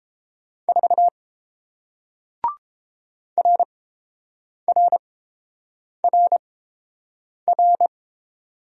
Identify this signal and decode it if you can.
Morse code, heard as a keyed tone, about 31 wpm, spaced as for 7 wpm, 700 Hz. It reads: 4 RRRR